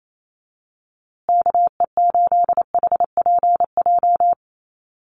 Morse code, heard as a keyed tone, about 28 wpm, 700 Hz. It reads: KE8HPJ
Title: USA Call Signs Repeated 3x 28wpm